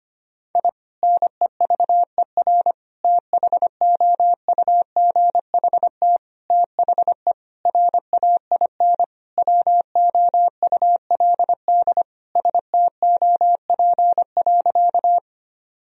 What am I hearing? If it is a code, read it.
Morse code, heard as a keyed tone, about 25 wpm, 700 Hz. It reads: I NEVER THOUGHT THE RAIN WOULD STOP.
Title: Sentences from Top 1000 Words - Review Part 1 - 25wpm